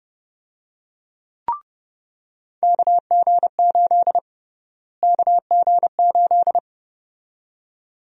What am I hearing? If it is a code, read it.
Morse code, heard as a keyed tone, about 30 wpm, 700 Hz. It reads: KG8 KG8
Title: USA Call Sign Prefixes - Repeated 2x - 30wpm